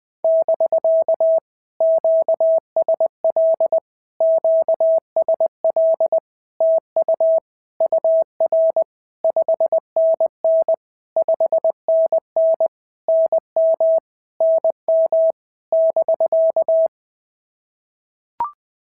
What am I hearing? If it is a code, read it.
Morse code, heard as a keyed tone, about 20 wpm, 650 Hz. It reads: <BK> QSL QSL TU UR 5NN 5NN NM NM <BK>
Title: POTA QSO Practice - 20wpm